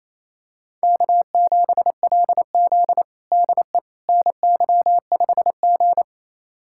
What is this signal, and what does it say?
Morse code, heard as a keyed tone, about 27 wpm, 700 Hz. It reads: K7LZ DE NY5G
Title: USA Callsign DE Callsign - 4 Characters - 28wpm